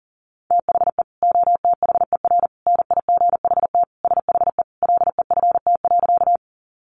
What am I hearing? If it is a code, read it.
Morse code, heard as a keyed tone, about 40 wpm, 700 Hz. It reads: THE OTHER NIGHT SHE LEFT.
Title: Sentences from Top 300 Words 40wpm